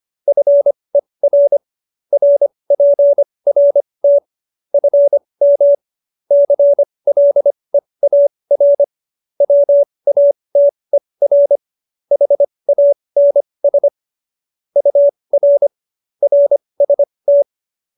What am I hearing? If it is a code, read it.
Morse code, heard as a keyed tone, about 25 wpm, spaced as for 20 wpm, 570 Hz. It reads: FER RPRT FM CLEAR WATER HANS UR RST